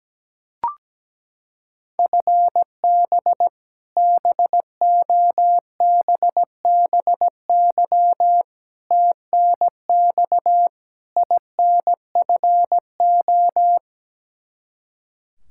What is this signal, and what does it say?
Morse code, heard as a keyed tone, about 17 wpm, 700 Hz. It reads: FB BOBBY TNX INFO
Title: Practice QSO - Exchange 3 - More Info - 17wpm